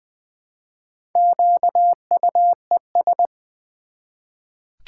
Morse code, heard as a keyed tone, about 20 wpm, 700 Hz. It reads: QUES
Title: Build A Word - 1 Letter At A Time - 5-Letter Words - 20wpm